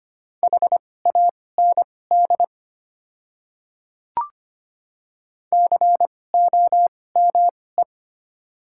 Morse code, heard as a keyed tone, about 25 wpm, spaced as for 18 wpm, 700 Hz. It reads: HAND COME
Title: 2x Character Spacing - Top 200 Words - 25wpm